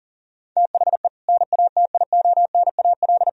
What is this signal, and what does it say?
Morse code, heard as a keyed tone, about 40 wpm, 700 Hz. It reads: THE NATIONAL
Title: Sentences from Top 1000 Words - Review Part 4 - 40wpm